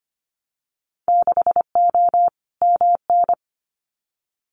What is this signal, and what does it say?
Morse code, heard as a keyed tone, about 25 wpm, 700 Hz. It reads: 6O MN
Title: Winter Field Day Exchanges Encore 25wpm